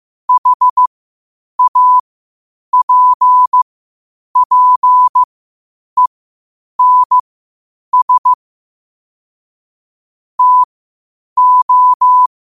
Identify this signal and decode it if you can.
Morse code, heard as a keyed tone, about 15 wpm, spaced as for 8 wpm, 1000 Hz. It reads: HAPPENS TO
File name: qotdfiles / 8wpm